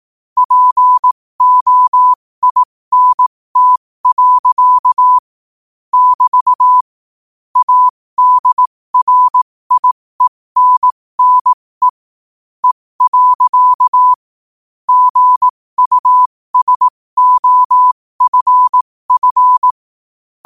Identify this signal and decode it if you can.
Morse code, heard as a keyed tone, about 18 wpm, spaced as for 15 wpm, 1000 Hz. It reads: POINT. <BT> ADRIENNE E. GUSOFF
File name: 15wpm